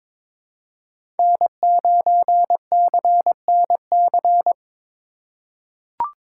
Morse code, heard as a keyed tone, about 22 wpm, 700 Hz. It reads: N9CNC
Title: USA Call Signs - 5 Characters - 4x Character Spacing - 22wpm